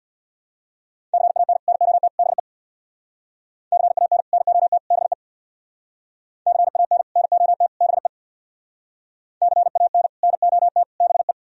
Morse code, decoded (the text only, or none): CAN NOT BE CAN NOT BE CAN NOT BE CAN NOT BE